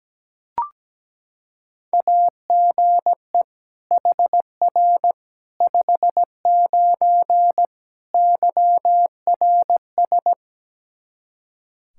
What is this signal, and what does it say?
Morse code, heard as a keyed tone, about 17 wpm, 700 Hz. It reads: AGE HR 59 YRS
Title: K1USN SST QSO Practice 17wpm